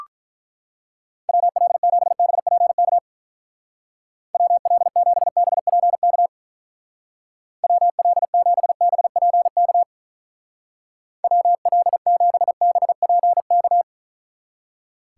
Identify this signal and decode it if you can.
Morse code, heard as a keyed tone, about 43 wpm, 700 Hz. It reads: WL7BPK WL7BPK WL7BPK WL7BPK
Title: Speed Racing - USA 2x3 Call Signs 35wpm